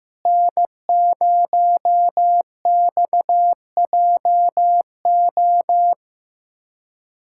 Morse code, heard as a keyed tone, about 15 wpm, 700 Hz. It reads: N0XJO
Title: USA Callsign DE Callsign - 5 Characters - 15wpm